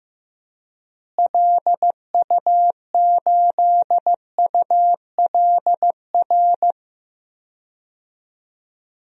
Morse code, heard as a keyed tone, about 15 wpm, 700 Hz. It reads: LU8ULR
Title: DXCC-100 Call Signs - 6 Characters - Repeated 3x - 15wpm